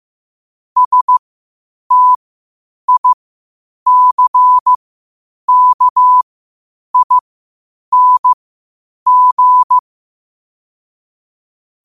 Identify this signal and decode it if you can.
Morse code, heard as a keyed tone, about 15 wpm, spaced as for 8 wpm, 1000 Hz. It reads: STICKING
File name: qotdfiles / 8wpm